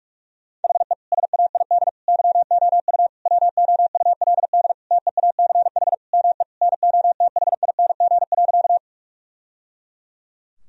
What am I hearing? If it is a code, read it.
Morse code, heard as a keyed tone, about 41 wpm, 700 Hz. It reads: HE SAID YOU WOULD TEACH ME NOTHING.
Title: Sentences from Top 1000 Words - Review Part 2 - 45wpm